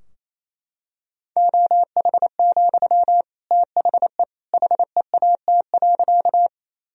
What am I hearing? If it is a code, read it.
Morse code, heard as a keyed tone, about 28 wpm, 700 Hz. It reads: OH, THE HEAT.
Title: Sentences from Top 1000 Words - Review Part 3 - 28wpm